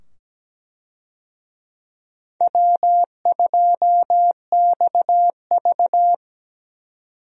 Morse code, heard as a keyed tone, about 17 wpm, 700 Hz. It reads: W2XV
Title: USA Call Signs - 4 Characters - 17wpm